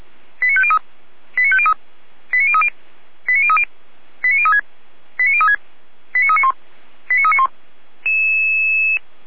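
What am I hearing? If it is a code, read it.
Five-tone selective call (ZVEI), heard as quick runs of five tones, tones of about 70 ms. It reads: 89503 89503 89030 89030 89036 89036 89391 89391
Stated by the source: ZVEI five-tone call received on an SDR.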